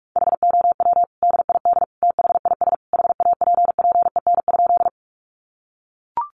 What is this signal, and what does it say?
Morse code, heard as a keyed tone, about 44 wpm, 700 Hz. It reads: HOW DID THIS HAPPEN?
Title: Sentences from Top 400 Words 45wpm